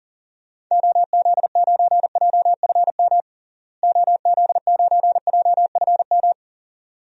Morse code, heard as a keyed tone, about 40 wpm, 700 Hz. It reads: OZ9JFM OZ9JFM
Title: DXCC-100 Call Signs - 6 Characters - Repeated 2x - 40wpm